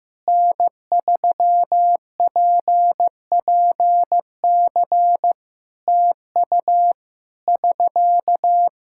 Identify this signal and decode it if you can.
Morse code, heard as a keyed tone, about 15 wpm, 700 Hz. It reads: N3PPC TU <SK>
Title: Practice QSO - Exchange 4 - Ending - 15wpm